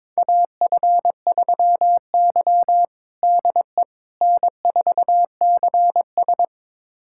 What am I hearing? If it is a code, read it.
Morse code, heard as a keyed tone, about 22 wpm, 700 Hz. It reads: AF3Y DE N4CS
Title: USA Callsign DE Callsign - 4 Characters - 22wpm